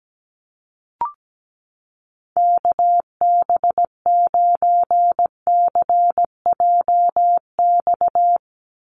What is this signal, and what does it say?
Morse code, heard as a keyed tone, about 17 wpm, 700 Hz. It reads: KB9CJX
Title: USA Call Signs 17wpm